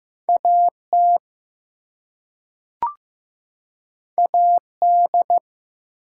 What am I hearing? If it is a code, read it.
Morse code, heard as a keyed tone, about 15 wpm, 700 Hz. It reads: AT AD
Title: Two Letter Words - 15wpm